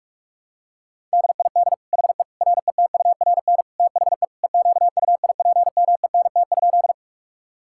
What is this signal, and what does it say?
Morse code, heard as a keyed tone, about 47 wpm, 700 Hz. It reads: DID HE RETURN THE EQUIPMENT?
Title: Sentences from Top 1000 Words - Review Part 5 - 45wpm